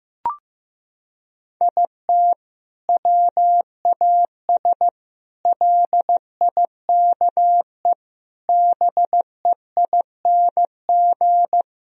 Morse code, heard as a keyed tone, about 16 wpm, 700 Hz. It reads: IT WAS LIKE BEING
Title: Sentences from Top 1000 Words - Review Part 3 - 15wpm